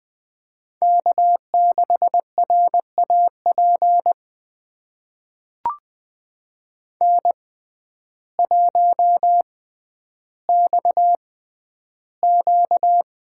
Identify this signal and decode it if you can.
Morse code, heard as a keyed tone, about 20 wpm, spaced as for 11 wpm, 700 Hz. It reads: K6RAP N1XQ
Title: USA Call Signs - 6x Character Spacing - 20wpm